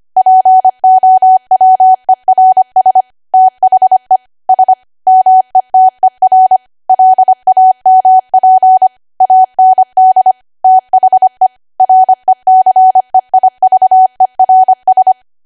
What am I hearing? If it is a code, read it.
Morse code, heard as a keyed tone, about 25 wpm, 750 Hz. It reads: POWERS THE S METER LAMP AND THE RECEIVERS
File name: Archive / 25 WPM / 25WPM